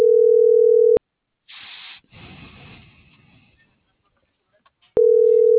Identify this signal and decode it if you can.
Ringback tone (North America), 440 + 480 Hz.